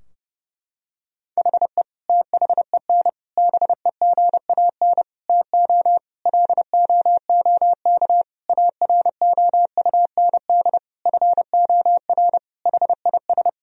Morse code, heard as a keyed tone, about 32 wpm, 700 Hz. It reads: HE THEN BEGAN TO LOOK AROUND FOR HIS
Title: Sentences from Top 1000 Words - Review Part 3 - 30wpm